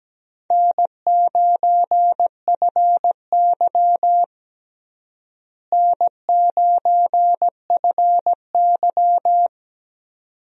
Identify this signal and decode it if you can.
Morse code, heard as a keyed tone, about 17 wpm, 700 Hz. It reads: N9FY N9FY
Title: USA Call Signs - 4 Characters - Repeated 2x - 17wpm